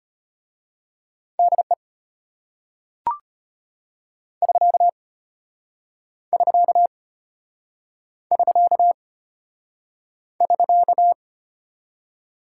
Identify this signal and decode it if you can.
Morse code, heard as a keyed tone, about 33 wpm, 700 Hz. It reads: DE <SK> <SK> <SK> <SK>